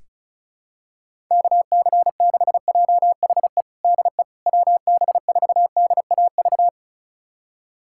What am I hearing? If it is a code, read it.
Morse code, heard as a keyed tone, about 34 wpm, 700 Hz. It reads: KC6JHE DE WB4DAV
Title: USA Callsign DE Callsign 35wpm